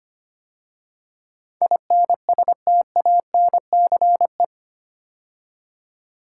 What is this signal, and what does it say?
Morse code, heard as a keyed tone, about 25 wpm, 700 Hz. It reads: INSTANCE
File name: 8 Letter Words - First 3 Letters Given - 25wpm